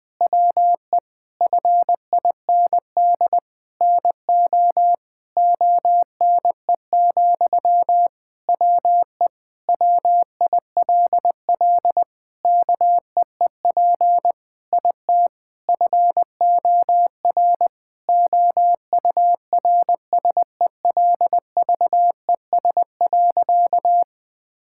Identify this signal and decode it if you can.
Morse code, heard as a keyed tone, about 20 wpm, 700 Hz. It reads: WE FIND NO ONE, WE WILL KEEP IT FOR OURSELVES.
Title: Sentences from Top 1600 Words 20wpm